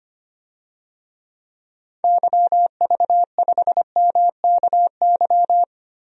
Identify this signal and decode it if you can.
Morse code, heard as a keyed tone, about 25 wpm, 700 Hz. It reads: YV5MKY
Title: DXCC-100 Call Signs - 6 Characters - Repeated 3x - 25wpm